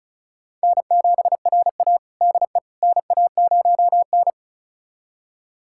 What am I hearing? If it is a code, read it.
Morse code, heard as a keyed tone, about 34 wpm, 700 Hz. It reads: N7RA DE NA0N
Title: USA Callsign DE Callsign - 4 Characters - 35wpm